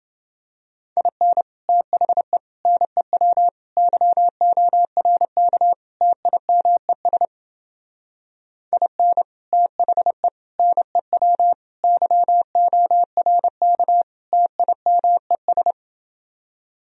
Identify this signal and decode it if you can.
Morse code, heard as a keyed tone, about 28 wpm, 700 Hz. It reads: IN THE NEW YORK TIMES IN THE NEW YORK TIMES